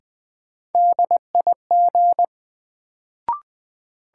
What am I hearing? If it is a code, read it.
Morse code, heard as a keyed tone, about 21 wpm, 700 Hz. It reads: DIG
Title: Sentences from Top 1600 Words - New Word Drill - 20wpm